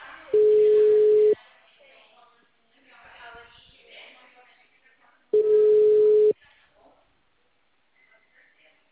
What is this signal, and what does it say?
Ringback tone (Europe, 425 Hz), with speech.
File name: out- -2055- - - .68731